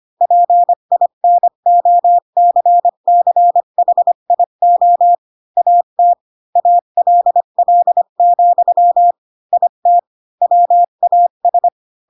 Morse code, heard as a keyed tone, about 25 wpm, 700 Hz. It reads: PINOCCHIO AT ALL, IT WAS
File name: Pinocchio / Pinocchio ch5 20 25 ik1hge